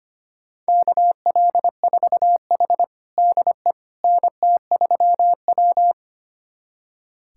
Morse code, heard as a keyed tone, about 25 wpm, 700 Hz. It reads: KL4H DE NT3W
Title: USA Callsign DE Callsign - 4 Characters - 25wpm